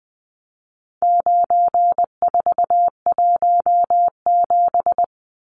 Morse code, heard as a keyed tone, about 20 wpm, 700 Hz. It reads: 9417
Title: Numbers - 4-Digit 20wpm